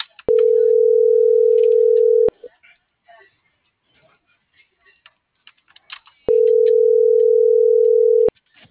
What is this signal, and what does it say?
Ringback tone (North America), 440 + 480 Hz.
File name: CENTRAL MULTIMODAL / out josimar.cruz - .61299